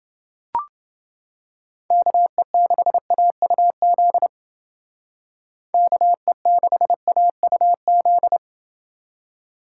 Morse code, heard as a keyed tone, about 29 wpm, 700 Hz. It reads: KE6AUZ KE6AUZ